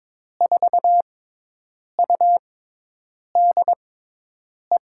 Morse code, heard as a keyed tone, about 22 wpm, spaced as for 8 wpm, 700 Hz. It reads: 4UDE
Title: USA Call Signs - 6 Characters - 6x Character Spacing - 22wpm